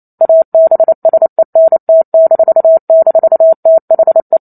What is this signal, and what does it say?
Morse code, heard as a keyed tone, about 28 wpm, 650 Hz. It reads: ABSENT--THE
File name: Chapter 2 - 28 WPM